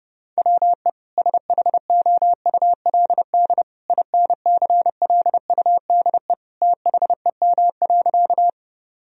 Morse code, heard as a keyed tone, about 30 wpm, 700 Hz. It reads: WE SHOULD INCLUDE THEM.
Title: Sentences from Top 1000 Words - Review Part 3 - 30wpm